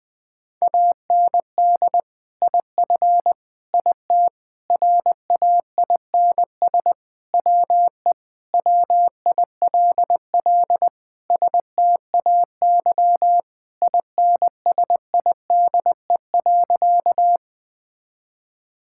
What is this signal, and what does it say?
Morse code, heard as a keyed tone, about 20 wpm, 700 Hz. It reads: AND IF IT RAINS WE WILL STAY INSIDE.
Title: Sentences with 2-Word Phrases - Spoken Beginning - 20wpm